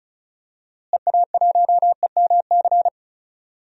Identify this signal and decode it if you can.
Morse code, heard as a keyed tone, about 36 wpm, 700 Hz. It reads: EA1EMC
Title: DXCC-100 Call Signs - 6 Characters - Repeated 3x - 35wpm